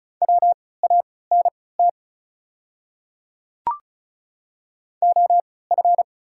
Morse code, heard as a keyed tone, about 36 wpm, spaced as for 20 wpm, 690 Hz. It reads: WANT OF
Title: 3x Character Spacing - Top 300 Words - 35wpm